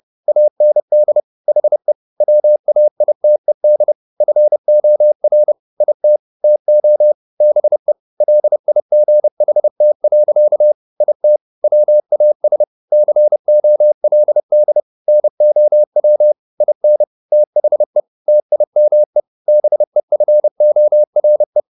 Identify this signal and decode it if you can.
Morse code, heard as a keyed tone, about 30 wpm, 600 Hz. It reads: AND HE WAITED FOR IT TO BE LIGHT. IT WAS COLD NOW IN THE TIME BEFORE
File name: the old man and the sea / the old man and the sea 1 30wpm